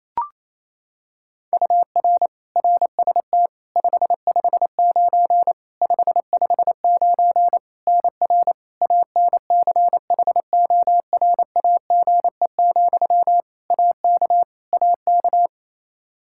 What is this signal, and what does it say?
Morse code, heard as a keyed tone, about 28 wpm, 700 Hz. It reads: UR RST 559 559 NR ANCHORAGE, AK AK
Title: Practice QSO - Exchange 1 - Beginning - 28wpm